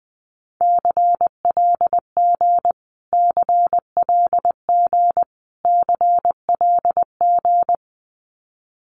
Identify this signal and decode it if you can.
Morse code, heard as a keyed tone, about 19 wpm, 700 Hz. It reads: CLG CLG CLG
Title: Abbreviations Repeated 3x - 20wpm